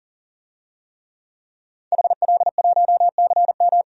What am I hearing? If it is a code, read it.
Morse code, heard as a keyed tone, about 37 wpm, 700 Hz. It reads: HL1CM
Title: DXCC-100 Call Signs - 5 Characters - 40wpm